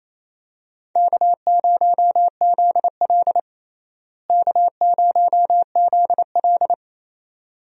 Morse code, heard as a keyed tone, about 27 wpm, 700 Hz. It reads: K0ZL K0ZL
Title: USA Call Signs - 4 Characters - Repeated 2x - 28wpm